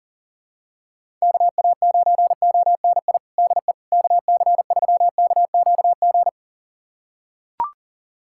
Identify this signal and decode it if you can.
Morse code, heard as a keyed tone, about 40 wpm, 700 Hz. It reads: KA9ONI DE KC3KQG
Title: USA Callsign DE Callsign - 6 Characters - 40wpm